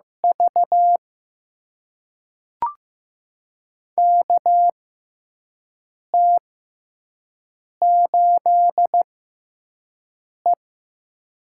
Morse code, heard as a keyed tone, about 15 wpm, spaced as for 5 wpm, 700 Hz. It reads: V KT8E
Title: USA Call Signs - 4 Characters - 6x Character Spacing - 15wpm